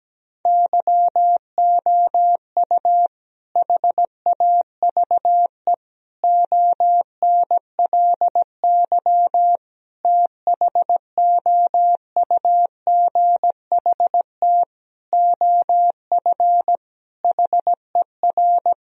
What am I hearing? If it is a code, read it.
Morse code, heard as a keyed tone, about 17 wpm, 700 Hz. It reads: YOU HAVE ONLY THOUGHT OF HER
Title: Sentences from Top 1000 Words - Review Part 2 - 17wpm